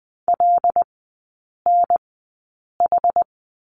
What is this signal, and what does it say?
Morse code, heard as a keyed tone, about 20 wpm, spaced as for 8 wpm, 700 Hz. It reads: LNH
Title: 198 Lesson 32e - Letter k - Three Characters 20wpm